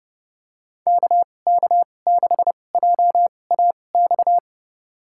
Morse code, heard as a keyed tone, about 30 wpm, spaced as for 22 wpm, 700 Hz. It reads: KK6JAX
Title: USA Call Signs - 6 Characters - 2x Character Spacing - 30wpm